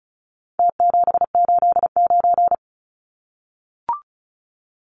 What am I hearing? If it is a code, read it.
Morse code, heard as a keyed tone, about 34 wpm, 700 Hz. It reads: T789
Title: Numbers - Contest Style - 4-Digit 35wpm